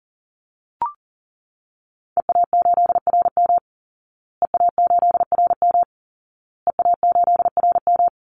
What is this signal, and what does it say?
Morse code, heard as a keyed tone, about 40 wpm, 700 Hz. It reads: EA8RM EA8RM EA8RM
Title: Call Signs Repeated 2x - 40wpm